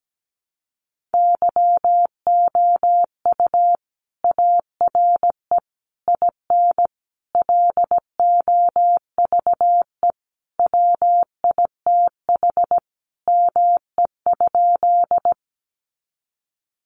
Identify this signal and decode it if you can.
Morse code, heard as a keyed tone, about 17 wpm, 700 Hz. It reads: YOU ARE IN LOVE WITH ME?
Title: Sentences from Top 400 Words 17wpm